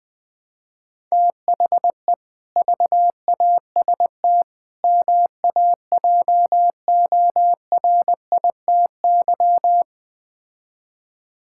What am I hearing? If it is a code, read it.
Morse code, heard as a keyed tone, about 21 wpm, 700 Hz. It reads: THE VAST MAJORITY
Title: Build A Phrase - 4-Word Phrases - 20wpm